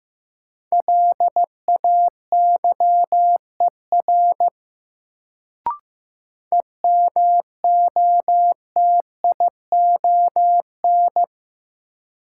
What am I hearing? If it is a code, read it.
Morse code, heard as a keyed tone, about 15 wpm, 700 Hz. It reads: LAYER EMOTION
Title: Sentences from Top 1600 Words - New Word Drill Alt - 15wpm